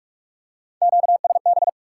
Morse code, heard as a keyed tone, about 47 wpm, 700 Hz. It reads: QSB
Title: Top 90 QSO Elements 45wpm